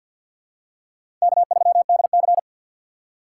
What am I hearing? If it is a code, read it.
Morse code, heard as a keyed tone, about 50 wpm, 700 Hz. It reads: K3DC
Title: USA Call Signs - 4x Character Spacing - 50wpm